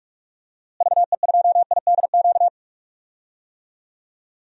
Morse code, heard as a keyed tone, about 42 wpm, 700 Hz. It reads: VE2IDQ
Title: DXCC-100 Call Signs - 6 Characters - Repeated 2x - 45wpm